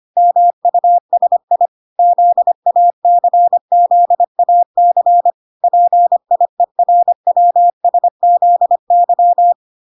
Morse code, heard as a keyed tone, about 25 wpm, 700 Hz. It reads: MUSI ZACZAC PIERWSZY